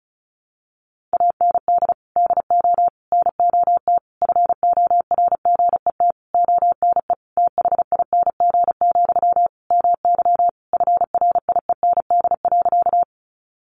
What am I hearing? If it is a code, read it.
Morse code, heard as a keyed tone, about 34 wpm, 700 Hz. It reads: AND DO NOT FORGET ONE THING, MY FRIEND.
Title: Sentences from Top 1200 Words 35wpm